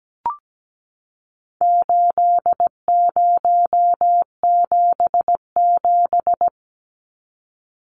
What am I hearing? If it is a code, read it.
Morse code, heard as a keyed tone, about 17 wpm, 700 Hz. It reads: 8077
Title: Numbers - 4-Digit 17wpm